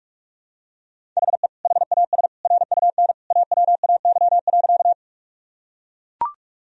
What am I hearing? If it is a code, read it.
Morse code, heard as a keyed tone, about 42 wpm, 700 Hz. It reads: HE HAS RUN AWAY.
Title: Sentences from Top 1000 Words - Review Part 1 - 45wpm